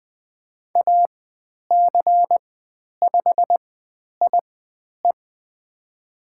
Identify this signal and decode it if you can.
Morse code, heard as a keyed tone, about 20 wpm, spaced as for 10 wpm, 700 Hz. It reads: AC5IE
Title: 102 Lesson 17g - Letter u - Call Signs 20wpm